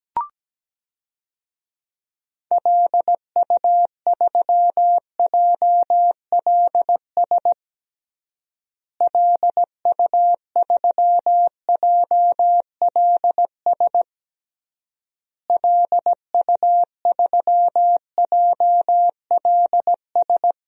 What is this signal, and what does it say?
Morse code, heard as a keyed tone, about 17 wpm, 700 Hz. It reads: LU3JLS LU3JLS LU3JLS
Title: DXCC-100 Call Signs - 6 Characters - Repeated 3x - 17wpm